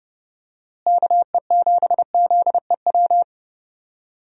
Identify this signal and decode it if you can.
Morse code, heard as a keyed tone, about 32 wpm, 700 Hz. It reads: KE7ZEW
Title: USA Call Signs - 6 Characters - Repeated 3x - 30wpm